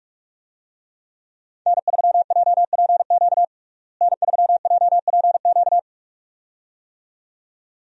Morse code, heard as a keyed tone, about 47 wpm, 700 Hz. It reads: N3JPQ N3JPQ
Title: USA Call Signs - 5 Characters - Repeated 2x - 45wpm